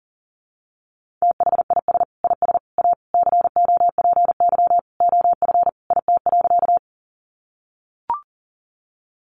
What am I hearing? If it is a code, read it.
Morse code, heard as a keyed tone, about 40 wpm, 700 Hz. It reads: THIS IS A COPY OF IT.
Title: Sentences from Top 1400 Words 40wpm